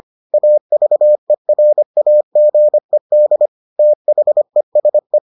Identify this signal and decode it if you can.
Morse code, heard as a keyed tone, about 25 wpm, 600 Hz. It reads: AVERAGED THESE